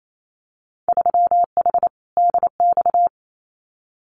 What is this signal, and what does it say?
Morse code, heard as a keyed tone, about 28 wpm, 700 Hz. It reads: 3H DX
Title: Winter Field Day Exchanges Encore 28wpm